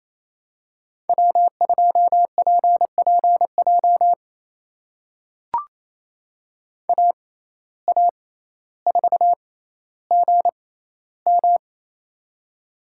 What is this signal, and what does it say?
Morse code, heard as a keyed tone, about 28 wpm, spaced as for 14 wpm, 700 Hz. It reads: W2PPJ AA4GM
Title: USA Call Signs - 5 Characters - 6x Character Spacing - 28wpm